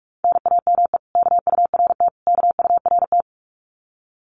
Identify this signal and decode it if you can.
Morse code, heard as a keyed tone, about 44 wpm, 700 Hz. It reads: NAME KURT KURT
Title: Common QSO Phrases Encore 45wpm